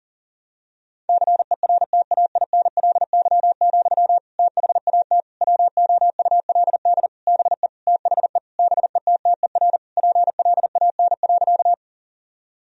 Morse code, heard as a keyed tone, about 40 wpm, 700 Hz. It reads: CERTAINLY, THAT WOULD BE THE BETTER PLAN.
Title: Sentences from Top 1000 Words - Review Part 2 - 40wpm